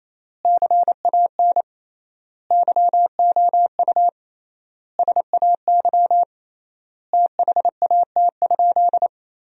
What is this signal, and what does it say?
Morse code, heard as a keyed tone, about 28 wpm, 700 Hz. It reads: CAN YOU SAY THAT?
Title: 3x Word-Spacing - Sentences from Top 200 Words - 28wpm